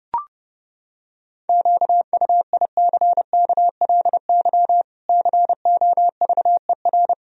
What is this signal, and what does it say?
Morse code, heard as a keyed tone, about 32 wpm, 700 Hz. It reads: QUICKLY COVER
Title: Sentences from Top 1000 Words - Review Part 1 - 30wpm